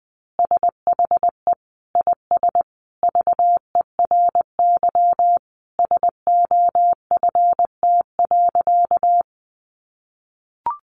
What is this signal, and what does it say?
Morse code, heard as a keyed tone, about 20 wpm, 700 Hz. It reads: SHE IS VERY SOFT.
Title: Easy Phrases 20wpm